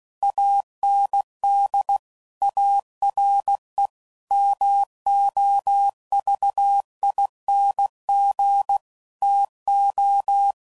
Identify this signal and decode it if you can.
Morse code, heard as a keyed tone, about 16 wpm, 800 Hz. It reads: AND ARE MOVING TO
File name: CW Code Groups @15WPM / QSOs / WD0BGP Spaced